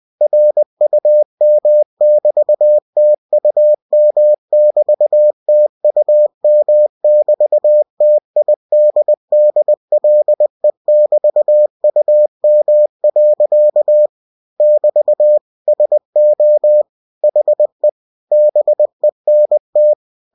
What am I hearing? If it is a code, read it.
Morse code, heard as a keyed tone, about 20 wpm, 600 Hz. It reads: RUM<BT>TUM<BT>TUM<BT>TIDDLE<BT>UM. <BT> SO HE BENT
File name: cw-books / winnie-the-pooh / winnie-20wpm-page-0013